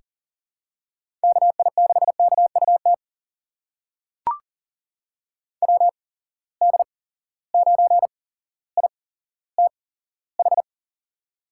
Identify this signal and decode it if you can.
Morse code, heard as a keyed tone, about 40 wpm, spaced as for 17 wpm, 700 Hz. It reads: KI6KUT WD9ITH
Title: USA Call Signs - 8x Character Spacing - 40wpm